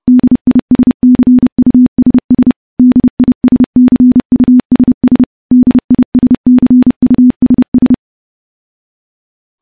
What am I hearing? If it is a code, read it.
Morse code, heard as a keyed tone, about 30 wpm, 250 Hz. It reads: DISCUSS DISCUSS DISCUSS
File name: morse / morse-training-30WPM